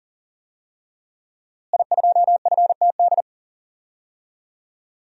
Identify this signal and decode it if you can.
Morse code, heard as a keyed tone, about 40 wpm, 700 Hz. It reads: I2FTD
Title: DXCC-100 Call Signs - 5 Characters - Repeated 3x - 40wpm